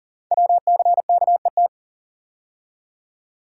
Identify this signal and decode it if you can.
Morse code, heard as a keyed tone, about 41 wpm, 700 Hz. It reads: WCKET
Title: Code Groups - 5 Letters - 40wpm